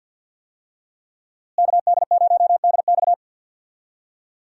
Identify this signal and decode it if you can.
Morse code, heard as a keyed tone, about 53 wpm, 700 Hz. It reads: KD0DX
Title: USA Call Signs - 5 Characters - Repeated 2x - 50wpm